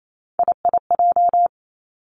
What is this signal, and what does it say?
Morse code, heard as a keyed tone, about 28 wpm, 700 Hz. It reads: IIJ
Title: Three Challenging Letter-Number Combinations 28wpm